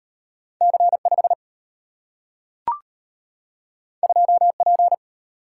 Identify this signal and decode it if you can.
Morse code, heard as a keyed tone, about 32 wpm, 700 Hz. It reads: C5 2P